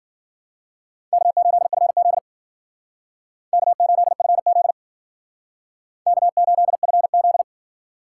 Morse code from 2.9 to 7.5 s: K8FZ K8FZ

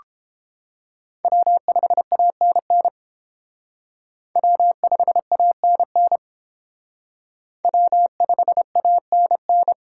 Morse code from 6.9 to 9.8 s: W5ANN

